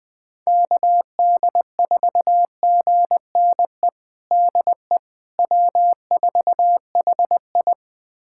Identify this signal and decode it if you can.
Morse code, heard as a keyed tone, about 20 wpm, 700 Hz. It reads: KD4GNE DE W4HI
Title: USA Callsign DE Callsign 20wpm